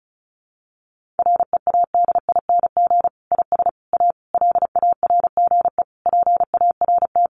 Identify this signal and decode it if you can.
Morse code, heard as a keyed tone, about 34 wpm, 700 Hz. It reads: READING IS A LARGE PART
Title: Sentences from Top 1400 Words 35wpm